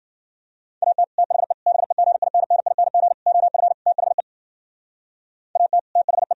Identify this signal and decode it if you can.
Morse code, heard as a keyed tone, about 52 wpm, 700 Hz. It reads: AT THE BEGINNING OF THE AT THE